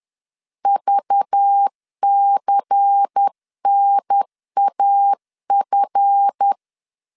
Morse code, heard as a keyed tone, about 10 wpm, 800 Hz. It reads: VCNAF
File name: LEZIONI / 7 GF9Z / 7 GF 10wpm 2514